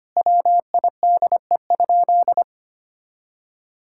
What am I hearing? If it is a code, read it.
Morse code, heard as a keyed tone, about 25 wpm, 700 Hz. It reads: WIDE?
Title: Sentences from Top 1000 Words - Review Part 6 - 25wpm